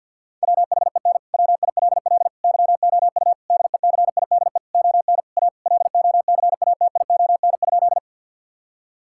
Morse code, heard as a keyed tone, about 50 wpm, 700 Hz. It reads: WHEN WILL YOU DECIDE ON A LOCATION?
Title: Sentences from Top 1000 Words - Review Part 6 - 50wpm